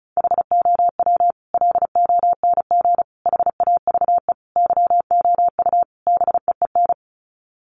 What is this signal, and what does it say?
Morse code, heard as a keyed tone, about 34 wpm, 700 Hz. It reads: HOW LONG HAVE YOU BEEN
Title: Sets of 5 Words Encore 35wpm